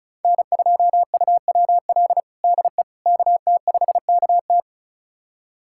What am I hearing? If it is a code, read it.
Morse code, heard as a keyed tone, about 36 wpm, 700 Hz. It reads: N2UWL DE KT5KT
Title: USA Callsign DE Callsign - 5 Characters - 35wpm